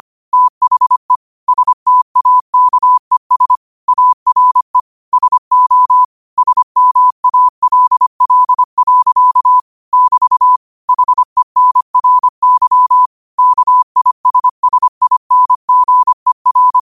Morse code, heard as a keyed tone, about 25 wpm, 1000 Hz. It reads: THE STAKES ARE SO SMALL. <BT> HENRY KISSINGER
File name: qotdfiles / 25wpm